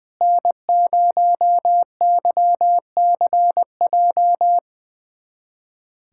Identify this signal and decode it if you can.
Morse code, heard as a keyed tone, about 20 wpm, 700 Hz. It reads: N0YCJ
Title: USA Callsign DE Callsign - 5 Characters - 20wpm